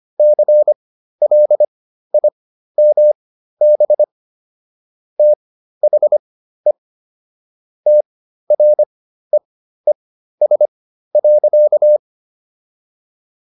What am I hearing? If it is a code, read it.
Morse code, heard as a keyed tone, about 25 wpm, spaced as for 13 wpm, 600 Hz. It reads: CLIMB THE TREES.